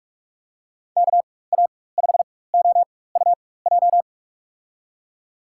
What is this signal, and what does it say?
Morse code, heard as a keyed tone, about 44 wpm, spaced as for 21 wpm, 700 Hz. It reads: KA5OUJ
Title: USA Call Signs - 6 Characters - 4x Character Spacing - 45wpm